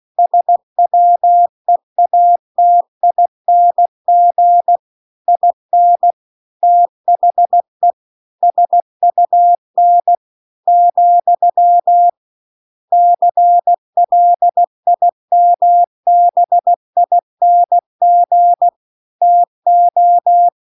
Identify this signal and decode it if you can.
Morse code, heard as a keyed tone, about 16 wpm, 700 Hz. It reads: SWEATING IN THE SUN , CLIMBING TO